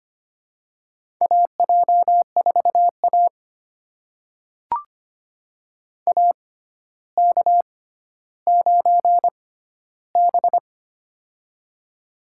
Morse code, heard as a keyed tone, about 25 wpm, spaced as for 13 wpm, 700 Hz. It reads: AJ4A AK9B
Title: USA Call Signs - 4 Characters - 6x Character Spacing - 25wpm